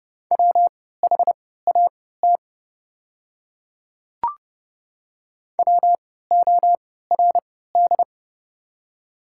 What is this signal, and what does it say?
Morse code, heard as a keyed tone, about 30 wpm, spaced as for 17 wpm, 700 Hz. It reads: WHAT WORD
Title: 3x Character Spacing - Top 100 Words - 30wpm